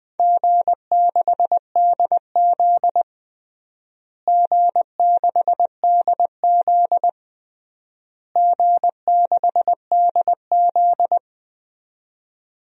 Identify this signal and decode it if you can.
Morse code, heard as a keyed tone, about 20 wpm, 700 Hz. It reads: G6DZ G6DZ G6DZ
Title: DXCC-100 Call Signs - 4 Characters - Repeated 3x - 20wpm